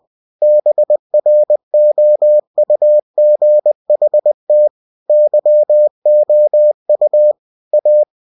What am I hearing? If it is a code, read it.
Morse code, heard as a keyed tone, about 20 wpm, 600 Hz. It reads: BROUGHT YOU A
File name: cw-books / winnie-the-pooh / winnie-20wpm-page-0057